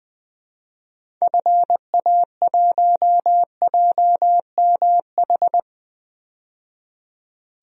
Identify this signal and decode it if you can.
Morse code, heard as a keyed tone, about 20 wpm, 700 Hz. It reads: FA1JMH
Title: DXCC-100 Call Signs - 6 Characters - 20wpm